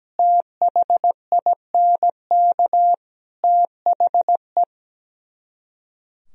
Morse code, heard as a keyed tone, about 17 wpm, 700 Hz. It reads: THINK THE
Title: Speed-Racing - Sets of 5 Words 17wpm